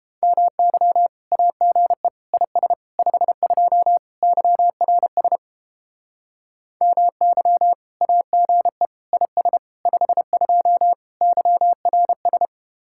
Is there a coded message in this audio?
Yes: Morse code from 0.2 to 12.5 s, heard as a keyed tone, about 29 wpm, 700 Hz. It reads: MY AGE IS 52 YRS MY AGE IS 52 YRS